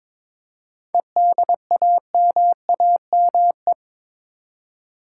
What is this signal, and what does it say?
Morse code, heard as a keyed tone, about 22 wpm, 690 Hz. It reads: EDAMAME
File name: Difficult Words with Hints - 22wpm